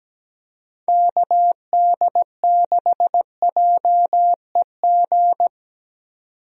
Morse code, heard as a keyed tone, about 17 wpm, 700 Hz. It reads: KD6JEG
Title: USA Call Signs - 6 Characters - 17wpm